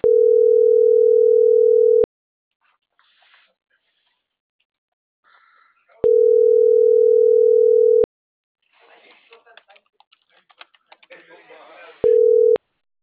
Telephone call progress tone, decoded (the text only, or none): ringback tone